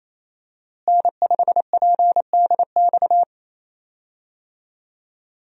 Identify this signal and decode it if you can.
Morse code, heard as a keyed tone, about 28 wpm, 700 Hz. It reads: N5PDX
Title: USA Call Signs - 5 Characters - Repeated 3x - 28wpm